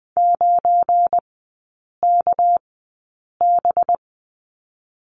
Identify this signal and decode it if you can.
Morse code, heard as a keyed tone, about 20 wpm, 700 Hz. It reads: 9 K B
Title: 198 Lesson 32e - Letter k - Three Characters 20wpm